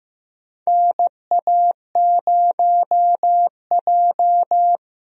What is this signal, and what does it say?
Morse code, heard as a keyed tone, about 16 wpm, 700 Hz. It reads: NA0J
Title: USA Call Signs - 4 Characters - Repeated 2x - 15wpm